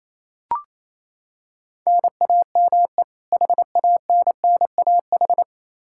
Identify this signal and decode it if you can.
Morse code, heard as a keyed tone, about 28 wpm, 700 Hz. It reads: NAME HANNAH
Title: Practice QSO - Exchange 1 - Beginning - 28wpm